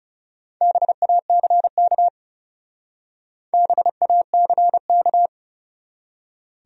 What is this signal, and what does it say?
Morse code, heard as a keyed tone, about 35 wpm, 700 Hz. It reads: BACK BACK
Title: Speed-Racing Top 300 Words 30wpm